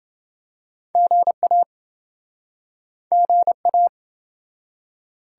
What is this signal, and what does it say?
Morse code, heard as a keyed tone, about 27 wpm, 700 Hz. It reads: GA GA